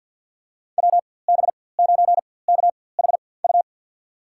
Morse code, heard as a keyed tone, about 52 wpm, spaced as for 23 wpm, 700 Hz. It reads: WB9KHU
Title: USA Call Signs - 6 Characters - 4x Character Spacing - 50wpm